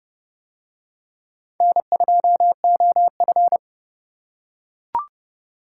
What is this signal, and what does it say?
Morse code, heard as a keyed tone, about 29 wpm, 700 Hz. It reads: N2OF